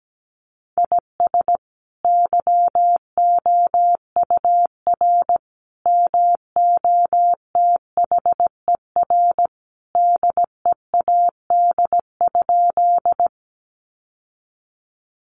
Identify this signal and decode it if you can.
Morse code, heard as a keyed tone, about 17 wpm, 700 Hz. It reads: IS YOUR MOTHER DEAD?
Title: Sentences from Top 1500 Words 17wpm